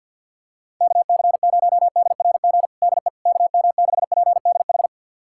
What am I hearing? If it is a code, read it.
Morse code, heard as a keyed tone, about 47 wpm, 700 Hz. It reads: KC0DRG DE KM6PDH
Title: USA Callsign DE Callsign - 6 Characters - 50wpm